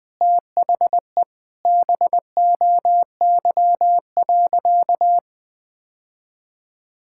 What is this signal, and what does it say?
Morse code, heard as a keyed tone, about 20 wpm, 700 Hz. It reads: THE BOY.
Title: Sentences from Top 1000 Words - Review Part 6 - 20wpm